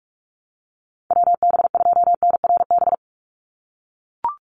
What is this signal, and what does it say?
Morse code, heard as a keyed tone, about 44 wpm, 700 Hz. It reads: WB2NRB
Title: USA Call Signs 45wpm